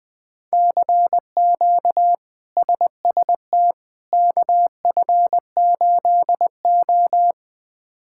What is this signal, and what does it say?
Morse code, heard as a keyed tone, about 21 wpm, 700 Hz. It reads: CQ SST KF8O
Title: K1USN SST QSO Practice - 20wpm